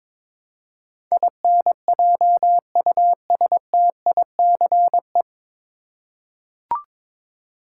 Morse code, heard as a keyed tone, about 21 wpm, 700 Hz. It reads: INJUSTICE
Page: Morse Code Ninja practice 22wpm